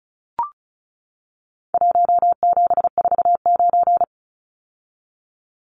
Morse code, heard as a keyed tone, about 34 wpm, 700 Hz. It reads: 1749
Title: Numbers - Contest Style - 4-Digit 35wpm